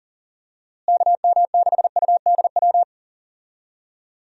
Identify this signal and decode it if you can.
Morse code, heard as a keyed tone, about 40 wpm, 700 Hz. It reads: KM6UDW
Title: USA Call Signs - 6 Characters - Repeated 3x - 40wpm